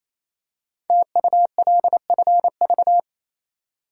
Morse code, heard as a keyed tone, about 27 wpm, 700 Hz. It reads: TULFV
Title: Code Groups - 5 Letters - 28wpm